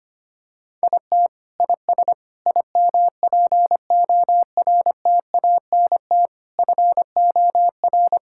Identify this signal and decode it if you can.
Morse code, heard as a keyed tone, about 25 wpm, 700 Hz. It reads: IT IS IMPORTANT FOR
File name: practice / 25wpm / Sentences with 4-Word Phrases - 25wpm